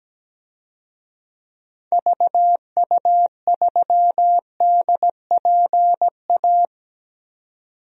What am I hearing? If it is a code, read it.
Morse code, heard as a keyed tone, about 17 wpm, 700 Hz. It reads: VU3DPA
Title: DXCC-100 Call Signs - 6 Characters - 17wpm